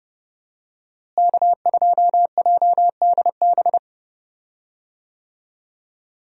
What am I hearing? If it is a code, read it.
Morse code, heard as a keyed tone, about 31 wpm, 700 Hz. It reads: K2JDB
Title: USA Call Signs - 5 Characters - Repeated 3x - 30wpm